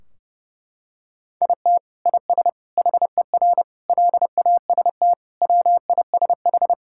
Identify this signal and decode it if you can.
Morse code, heard as a keyed tone, about 30 wpm, 700 Hz. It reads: IT IS HER LAST WISH
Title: Sentences from Top 1600 Words 30wpm